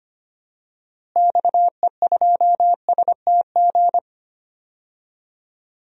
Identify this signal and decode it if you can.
Morse code, heard as a keyed tone, about 25 wpm, 700 Hz. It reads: XE2STG
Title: DXCC-100 Call Signs - 6 Characters - Repeated 2x - 25wpm